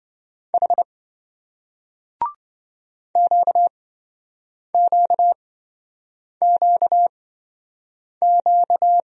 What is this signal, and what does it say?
Morse code, heard as a keyed tone, about 27 wpm, spaced as for 7 wpm, 700 Hz. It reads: H QQQQ